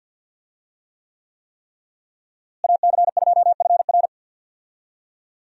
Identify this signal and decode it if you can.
Morse code, heard as a keyed tone, about 48 wpm, 700 Hz. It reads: AC2FR
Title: USA Call Signs - 5 Characters - Repeated 3x - 50wpm